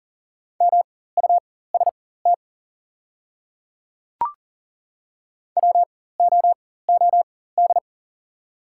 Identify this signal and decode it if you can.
Morse code, heard as a keyed tone, about 40 wpm, spaced as for 19 wpm, 700 Hz. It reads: MUST WOOD
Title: 4x Character Spacing - Top 300 Words - 40wpm